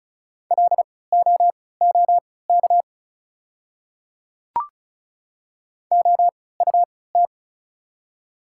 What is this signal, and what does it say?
Morse code, heard as a keyed tone, about 36 wpm, spaced as for 20 wpm, 700 Hz. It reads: LOOK OUT
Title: 3x Character Spacing - Top 100 Words - 35wpm